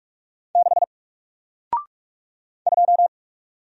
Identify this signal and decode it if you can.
Morse code, heard as a keyed tone, about 44 wpm, 700 Hz. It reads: 6 2
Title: Single Letter-Number Alt 45wpm